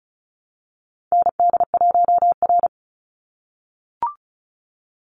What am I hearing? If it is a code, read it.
Morse code, heard as a keyed tone, about 34 wpm, 700 Hz. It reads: ND1R
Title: Call Signs - 35wpm